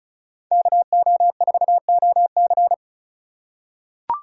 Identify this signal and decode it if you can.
Morse code, heard as a keyed tone, about 36 wpm, 700 Hz. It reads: KO4OC
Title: USA Call Signs - 5 Characters - 3x Character Spacing - 35wpm